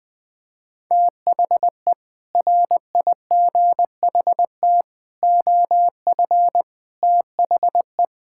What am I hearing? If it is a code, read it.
Morse code, heard as a keyed tone, about 21 wpm, 700 Hz. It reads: THE RIGHT OF THE
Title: Sentences with 6-Word Phrases - Spoken Beginning - 20wpm